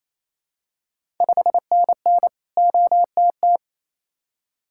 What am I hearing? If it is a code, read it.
Morse code, heard as a keyed tone, about 28 wpm, 700 Hz. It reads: 5NN OTT
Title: Ontario QSO Party Exchange 28wpm